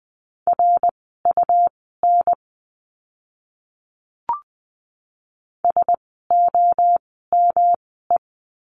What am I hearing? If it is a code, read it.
Morse code, heard as a keyed tone, about 20 wpm, spaced as for 14 wpm, 700 Hz. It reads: RUN SOME
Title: 237 Lesson 38f - Letter q - Words 20wpm